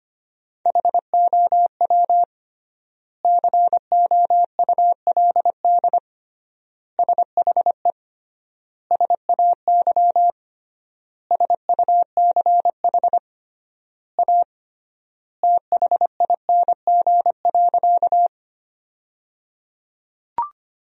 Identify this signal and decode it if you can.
Morse code, heard as a keyed tone, about 25 wpm, 700 Hz. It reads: HOW COULD SHE SAY SUCH A THING.
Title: 3x Word-Spacing - Sentences from Top 200 Words - 25wpm